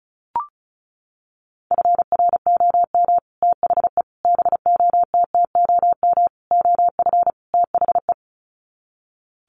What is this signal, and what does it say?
Morse code, heard as a keyed tone, about 34 wpm, 700 Hz. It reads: FROM THE BOTTOM OF THE
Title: Sets of 5 Words Encore2 35wpm